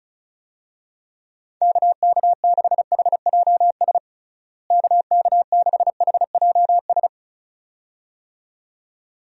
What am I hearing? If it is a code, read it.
Morse code, heard as a keyed tone, about 34 wpm, 700 Hz. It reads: KK6HJS KK6HJS
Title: USA Call Signs - 6 Characters - Repeated 2x - 35wpm